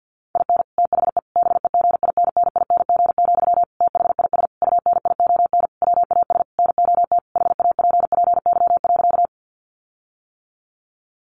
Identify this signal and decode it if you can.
Morse code, heard as a keyed tone, about 50 wpm, 700 Hz. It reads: IN THE BEGINNING, THIS UNION WAS NOT HAPPY.
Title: Sentences from Top 1400 Words 50wpm